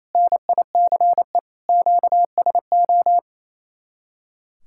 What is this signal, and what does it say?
Morse code, heard as a keyed tone, about 34 wpm, 700 Hz. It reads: NICE QSO
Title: Speed-Racing Common QSO Phrases - 28wpm